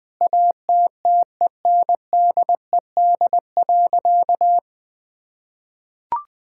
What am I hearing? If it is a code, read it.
Morse code, heard as a keyed tone, about 21 wpm, 700 Hz. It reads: ATTENDED.
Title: Sentences with 4-Word Phrases - 20wpm